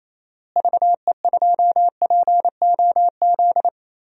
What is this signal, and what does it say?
Morse code, heard as a keyed tone, about 28 wpm, 700 Hz. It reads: VE2POZ
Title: DXCC-100 Call Signs - 6 Characters - Repeated 2x - 28wpm